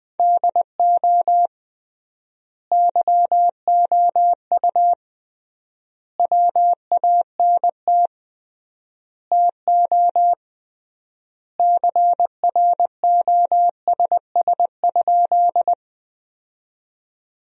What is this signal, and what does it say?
Morse code, heard as a keyed tone, about 21 wpm, 700 Hz. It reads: DO YOU WANT TO CROSS?
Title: 3x Word-Spacing - Sentences from Top 300 Words - 20wpm